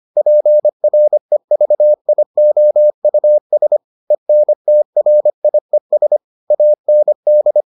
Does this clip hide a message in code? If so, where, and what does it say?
Yes: Morse code from 0.2 to 7.6 s, heard as a keyed tone, about 25 wpm, 600 Hz. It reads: PREVIOUS ENTRIES AND